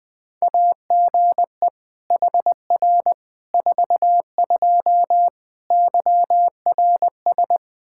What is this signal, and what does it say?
Morse code, heard as a keyed tone, about 20 wpm, 700 Hz. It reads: AGE HR 42 YRS
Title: Practice QSO - Exchange 3 - More Info - 20wpm